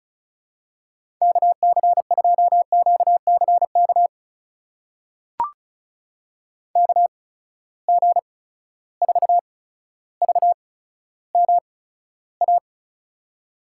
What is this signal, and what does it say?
Morse code, heard as a keyed tone, about 36 wpm, spaced as for 15 wpm, 700 Hz. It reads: KC2QCK KG4VMA
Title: USA Call Signs - 6 Characters - 8x Character Spacing - 35wpm